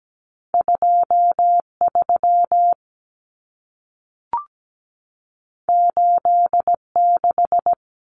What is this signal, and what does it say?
Morse code, heard as a keyed tone, about 17 wpm, 700 Hz. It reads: 23 86
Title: Common Cut Numbers - 2-Digit 17wpm